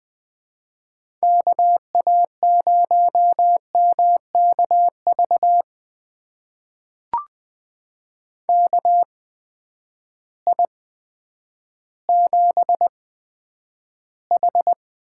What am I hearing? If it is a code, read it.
Morse code, heard as a keyed tone, about 20 wpm, spaced as for 10 wpm, 700 Hz. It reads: KA0MKV KI7H
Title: USA Call Signs - 6 Characters - 8x Character Spacing - 20wpm